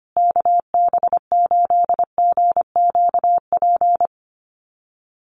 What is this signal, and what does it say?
Morse code, heard as a keyed tone, about 25 wpm, 700 Hz. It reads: KB8GQP
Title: USA Call Signs Repeated 3x 25wpm